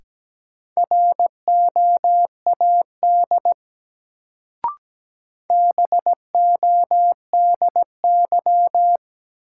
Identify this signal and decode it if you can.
Morse code, heard as a keyed tone, about 17 wpm, 700 Hz. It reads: ROAD BODY
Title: Sentences from Top 400 Words - New Word Drill - 17wpm